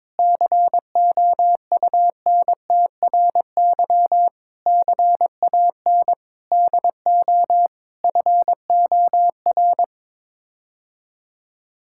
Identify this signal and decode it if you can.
Morse code, heard as a keyed tone, about 22 wpm, 700 Hz. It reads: COUNTRY CAN DO FOR
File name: practice / 22wpm / Sets of 8 Words - 22wpm